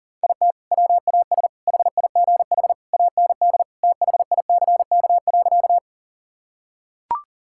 Morse code, heard as a keyed tone, about 37 wpm, 700 Hz. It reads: IT WAS HIGH AND THICK.
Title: Sentences from Top 1600 Words 40wpm